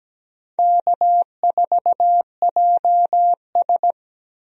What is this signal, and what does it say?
Morse code, heard as a keyed tone, about 17 wpm, 700 Hz. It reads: K4JS
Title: USA Call Signs - 4 Characters - 17wpm